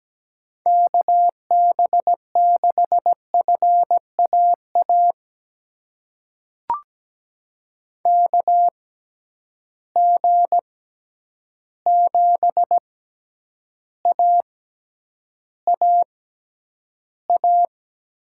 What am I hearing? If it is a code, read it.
Morse code, heard as a keyed tone, about 17 wpm, spaced as for 9 wpm, 700 Hz. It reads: KB6FAA KG7AAA
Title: USA Call Signs - 6 Characters - 6x Character Spacing - 17wpm